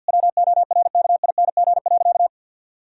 Morse code, heard as a keyed tone, about 50 wpm, 700 Hz. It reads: WORKING.
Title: Sentences from Top 1000 Words - Review Part 4 - 50wpm